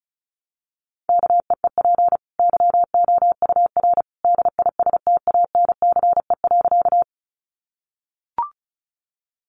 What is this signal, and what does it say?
Morse code, heard as a keyed tone, about 34 wpm, 700 Hz. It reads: KEEP YOUR DISTANCE.
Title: Sentences from Top 1200 Words 35wpm